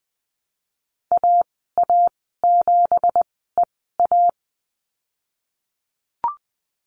Morse code, heard as a keyed tone, about 20 wpm, spaced as for 15 wpm, 700 Hz. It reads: AA7EA
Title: 179 Lesson 29g - Slash - Call Signs 20wpm